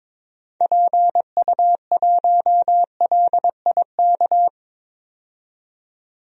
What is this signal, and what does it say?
Morse code, heard as a keyed tone, about 22 wpm, 700 Hz. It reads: PU1LIK
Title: DXCC-100 Call Signs - 6 Characters - 22wpm